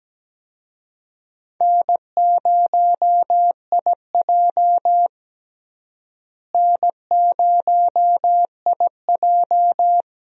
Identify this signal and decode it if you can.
Morse code, heard as a keyed tone, about 17 wpm, 700 Hz. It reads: N0IJ N0IJ
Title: USA Call Signs - 4 Characters - Repeated 3x - 17wpm